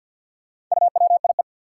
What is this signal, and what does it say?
Morse code, heard as a keyed tone, about 53 wpm, 700 Hz. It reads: UWIE
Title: Four Challenging Letter-Number Combinations 50wpm